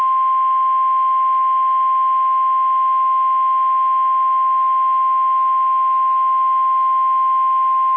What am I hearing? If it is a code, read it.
Emergency Alert System (SAME) tones (the 1050 Hz attention tone). It is heard from 0.2 to 7.6 s.
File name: eas / KSDZKDJL / KSDZKDJL